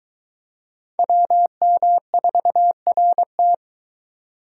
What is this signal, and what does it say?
Morse code, heard as a keyed tone, about 21 wpm, 700 Hz. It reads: WM4RT